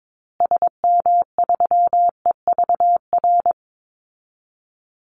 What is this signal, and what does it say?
Morse code, heard as a keyed tone, about 22 wpm, 700 Hz. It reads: SM3EVR
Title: Call Signs Repeated 2x - 22wpm